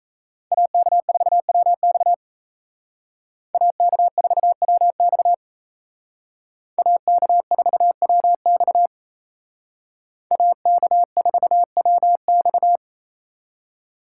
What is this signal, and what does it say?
Morse code, heard as a keyed tone, about 35 wpm, 700 Hz. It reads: AK4WX AK4WX AK4WX AK4WX